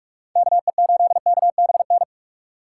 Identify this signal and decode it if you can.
Morse code, heard as a keyed tone, about 44 wpm, 700 Hz. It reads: KE8KBN
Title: USA Call Signs - 3x Character Spacing - 45wpm